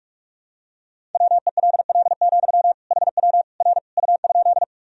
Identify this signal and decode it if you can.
Morse code, heard as a keyed tone, about 42 wpm, 700 Hz. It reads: WELL, HW R U?
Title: Sentences from Top 200 Words and Abbreviations 45wpm